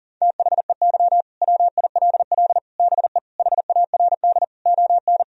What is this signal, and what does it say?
Morse code, heard as a keyed tone, about 37 wpm, 700 Hz. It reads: THEY WILL BE HARD ON
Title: Sentences from Top 1000 Words - Review Part 1 - 40wpm